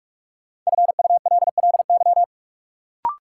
Morse code, heard as a keyed tone, about 46 wpm, 700 Hz. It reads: FULLY
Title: Sentences from Top 1300 Words - New Word Drill - 45wpm